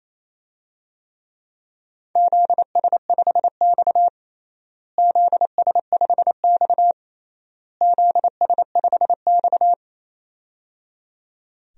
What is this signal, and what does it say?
Morse code, heard as a keyed tone, about 28 wpm, 700 Hz. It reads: ZS5X ZS5X ZS5X
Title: DXCC-100 Call Signs - 4 Characters - Repeated 3x - 28wpm